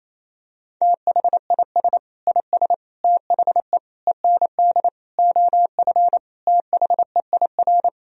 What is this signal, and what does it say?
Morse code, heard as a keyed tone, about 28 wpm, 700 Hz. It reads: THIS IS THE END OF THEIR
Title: Sentences from Top 1000 Words - Review Part 1 - 28wpm